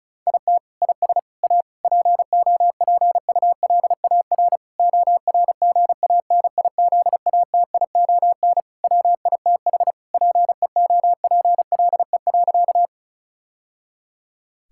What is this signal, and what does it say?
Morse code, heard as a keyed tone, about 34 wpm, 700 Hz. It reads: IT IS A POPULAR ORGANIZATION WITH PEOPLE.
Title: Sentences from Top 1000 Words - Review Part 5 - 35wpm